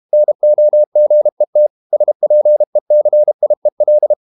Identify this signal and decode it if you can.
Morse code, heard as a keyed tone, about 32 wpm, 600 Hz. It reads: NOGET SPECIEL
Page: Ferie 32 wpm